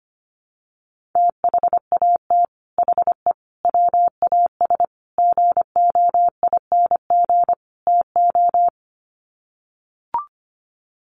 Morse code, heard as a keyed tone, about 25 wpm, 700 Hz. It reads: THAT HE WAS GOING TO
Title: Sets of 5 Words 25wpm